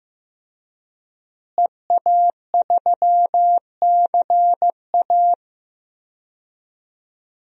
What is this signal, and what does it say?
Morse code, heard as a keyed tone, about 16 wpm, 700 Hz. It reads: EA3CA
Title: DXCC-100 Call Signs - 5 Characters - Repeated 3x - 15wpm